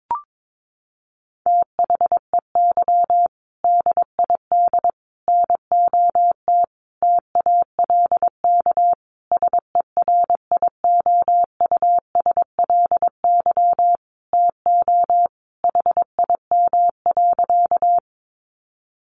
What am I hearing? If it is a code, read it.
Morse code, heard as a keyed tone, about 22 wpm, 700 Hz. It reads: THEY DID NOT TALK SERIOUSLY TO HIM.
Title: Sentences from Top 1200 Words 22wpm